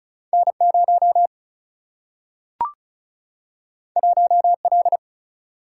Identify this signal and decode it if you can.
Morse code, heard as a keyed tone, about 36 wpm, 700 Hz. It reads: N 0 1 L
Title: Single-Letter Single-Number 35wpm